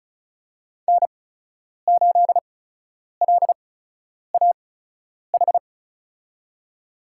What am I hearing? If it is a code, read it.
Morse code, heard as a keyed tone, about 36 wpm, spaced as for 10 wpm, 700 Hz. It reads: N8LAH
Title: USA Call Signs - 5 Characters - 8x Character Spacing - 35wpm